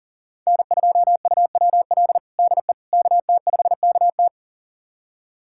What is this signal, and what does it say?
Morse code, heard as a keyed tone, about 40 wpm, 700 Hz. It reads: N2UWL DE KT5KT
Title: USA Callsign DE Callsign - 5 Characters - 40wpm